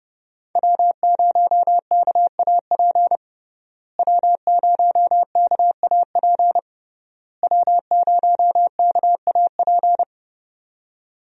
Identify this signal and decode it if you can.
Morse code, heard as a keyed tone, about 32 wpm, 700 Hz. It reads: W0KAP W0KAP W0KAP
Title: USA Call Signs - 5 Characters - Repeated 3x - 30wpm